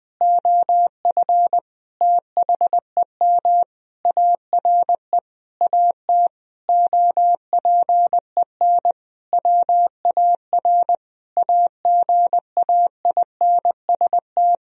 Morse code, heard as a keyed tone, about 20 wpm, 700 Hz. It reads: OF THEM ARE AT OPEN WAR AGAINST
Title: Sentences from Top 1000 Words - Review Part 2 - 20wpm